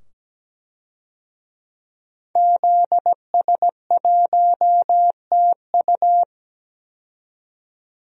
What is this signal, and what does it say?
Morse code, heard as a keyed tone, about 17 wpm, 700 Hz. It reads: ZS1TU
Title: DXCC-100 Call Signs - 5 Characters - 17wpm